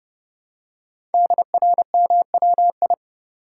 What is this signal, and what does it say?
Morse code, heard as a keyed tone, about 30 wpm, 700 Hz. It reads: DRMWI
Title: Code Groups - 5 Letters - 30wpm